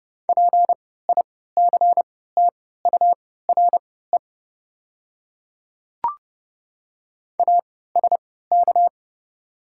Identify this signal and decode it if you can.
Morse code, heard as a keyed tone, about 29 wpm, spaced as for 17 wpm, 700 Hz. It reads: PICTURE ASK
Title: 3x Character Spacing - Top 200 Words - 30wpm